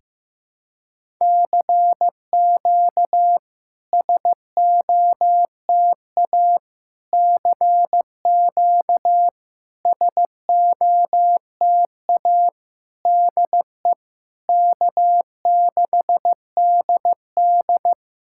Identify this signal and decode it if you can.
Morse code, heard as a keyed tone, about 16 wpm, 700 Hz. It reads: CQ SOTA CQ SOTA DE K6DD
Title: SOTA QSO Practice 15wpm